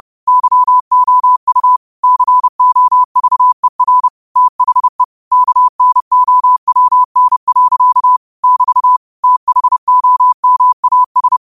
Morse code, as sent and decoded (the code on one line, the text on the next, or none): -.-- --- ..- / -.-. --- ...- . .-. / - .... / -.- -. --- .-- -. .-.-.- / -...- / - .... --- -- .- ...
YOU COVER THE KNOWN. <BT> THOMAS